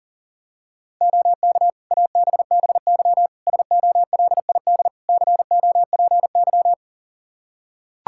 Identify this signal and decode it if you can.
Morse code, heard as a keyed tone, about 37 wpm, 700 Hz. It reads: OK ABBY SOLID COPY
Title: Practice QSO - Exchange 3 - More Info - 40wpm